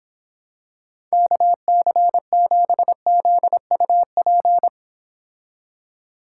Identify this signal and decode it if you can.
Morse code, heard as a keyed tone, about 26 wpm, 700 Hz. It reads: KC7ZUP